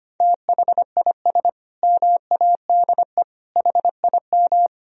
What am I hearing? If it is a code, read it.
Morse code, heard as a keyed tone, about 25 wpm, 700 Hz. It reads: THIS MADE HIM
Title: Sentences from Top 500 Words and Abbreviations 25wpm